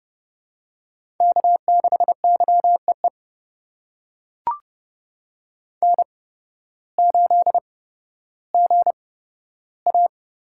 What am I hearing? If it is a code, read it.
Morse code, heard as a keyed tone, about 30 wpm, spaced as for 14 wpm, 700 Hz. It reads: K6YEE N8GA
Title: USA Call Signs - 5 Characters - 8x Character Spacing - 30wpm